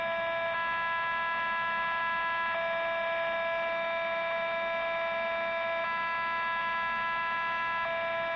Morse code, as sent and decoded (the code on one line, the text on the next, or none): none